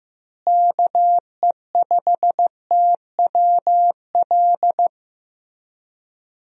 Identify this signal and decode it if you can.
Morse code, heard as a keyed tone, about 16 wpm, 700 Hz. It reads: KE5TWL
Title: USA Call Signs - 6 Characters - Repeated 3x - 15wpm